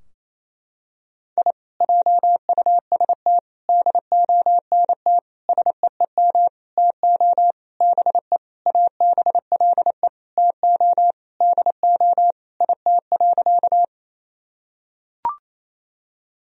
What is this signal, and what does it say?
Morse code, heard as a keyed tone, about 28 wpm, 700 Hz. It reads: I JUST DONT SEEM TO BE ABLE TO DO IT.
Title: Sentences from Top 1000 Words - Review Part 2 - 28wpm